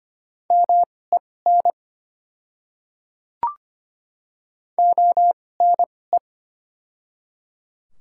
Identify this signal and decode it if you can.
Morse code, heard as a keyed tone, about 25 wpm, spaced as for 18 wpm, 700 Hz. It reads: MEN ONE
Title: 2x Character Spacing - Top 200 Words - 25wpm